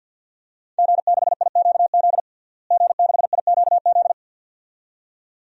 Morse code, heard as a keyed tone, about 47 wpm, 700 Hz. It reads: G6IQZ G6IQZ
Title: DXCC-100 Call Signs - 5 Characters - Repeated 2x - 50wpm